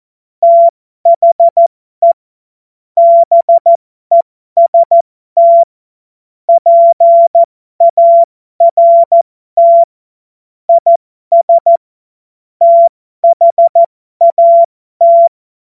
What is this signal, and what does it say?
Morse code, heard as a keyed tone, about 14 wpm, spaced as for 12 wpm, 680 Hz. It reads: THE BEST PART IS THAT
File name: Morsekurs / Witze